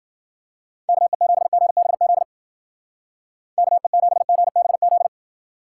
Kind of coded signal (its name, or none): Morse code